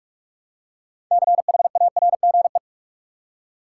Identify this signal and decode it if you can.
Morse code, heard as a keyed tone, about 46 wpm, 700 Hz. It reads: CHARGE
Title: Six Letter Words - 45wpm